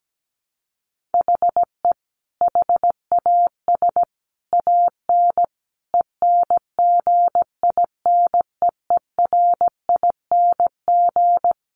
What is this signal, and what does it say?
Morse code, heard as a keyed tone, about 17 wpm, 700 Hz. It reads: HE HAS AN ENGINEERING
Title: Sentences from Top 1500 Words 17wpm